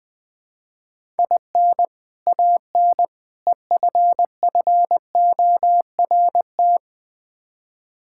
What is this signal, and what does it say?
Morse code, heard as a keyed tone, about 20 wpm, 700 Hz. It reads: IN AN EFFORT
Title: Build A Phrase - 4-Word Phrases - 20wpm